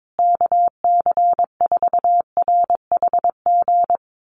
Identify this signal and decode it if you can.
Morse code, heard as a keyed tone, about 22 wpm, 700 Hz. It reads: KC4RHG
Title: USA Call Signs Repeated 2x 22wpm